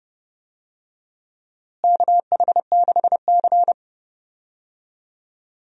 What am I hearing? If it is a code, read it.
Morse code, heard as a keyed tone, about 30 wpm, 700 Hz. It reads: KH6C
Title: DXCC-100 Call Signs - 4 Characters - Repeated 3x - 30wpm